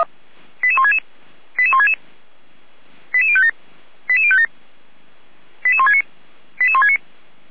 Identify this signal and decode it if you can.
Five-tone selective call (ZVEI), heard as quick runs of five tones, tones of about 70 ms. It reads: 88166 88166 88057 88057 88169 88169